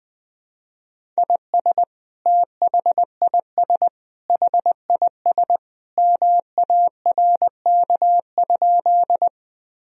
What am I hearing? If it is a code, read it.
Morse code, heard as a keyed tone, about 21 wpm, 700 Hz. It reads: IS THIS HIS MARK?
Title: Sentences from Top 1000 Words - Review Part 1 - 20wpm